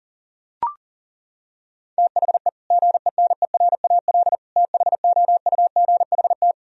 Morse code, heard as a keyed tone, about 37 wpm, 700 Hz. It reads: THE GENERAL THOUGHT
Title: Sentences from Top 1000 Words - Review Part 3 - 40wpm